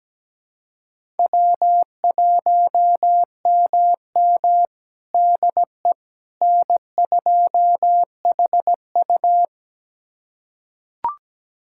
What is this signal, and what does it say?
Morse code, heard as a keyed tone, about 17 wpm, 700 Hz. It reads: W1MM DE N2HU
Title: USA Callsign DE Callsign - 4 Characters - 17wpm